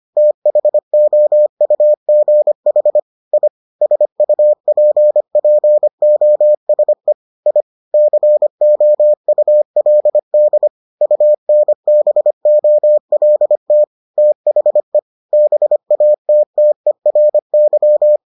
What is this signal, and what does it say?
Morse code, heard as a keyed tone, about 25 wpm, 600 Hz. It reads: THOUGH I SUPPOSE I COULD UNBOLT THE BATTERY